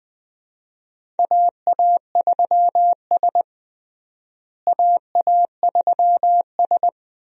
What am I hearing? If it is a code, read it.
Morse code, heard as a keyed tone, about 21 wpm, 700 Hz. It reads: AA3S AA3S
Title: USA Call Signs - 4 Characters - Repeated 2x - 20wpm